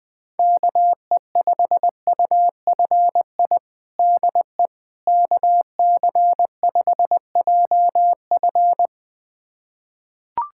Morse code, heard as a keyed tone, about 20 wpm, 700 Hz. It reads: KE5UFI DE KC5JF
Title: USA Callsign DE Callsign 20wpm